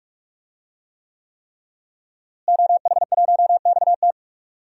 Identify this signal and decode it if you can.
Morse code, heard as a keyed tone, about 46 wpm, 700 Hz. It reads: OH1XT
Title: DXCC-100 Call Signs - 5 Characters - Repeated 3x - 45wpm